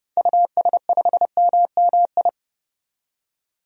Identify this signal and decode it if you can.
Morse code, heard as a keyed tone, about 30 wpm, 700 Hz. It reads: US5MMI
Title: DXCC-100 Call Signs - 6 Characters - 30wpm